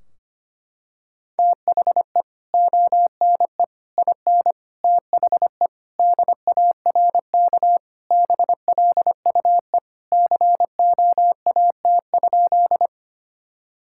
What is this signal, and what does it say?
Morse code, heard as a keyed tone, about 25 wpm, 700 Hz. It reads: THE ONE IN THE DARK BLUE COAT?
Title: Sentences from Top 1600 Words 25wpm